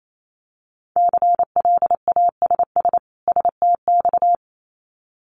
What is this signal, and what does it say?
Morse code, heard as a keyed tone, about 28 wpm, 700 Hz. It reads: CLASS STX
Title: Winter Field Day Exchanges 28wpm